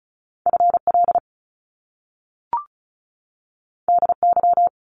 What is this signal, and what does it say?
Morse code, heard as a keyed tone, about 36 wpm, 700 Hz. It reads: FL DY
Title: Top Two Letter Combinations in Words 35wpm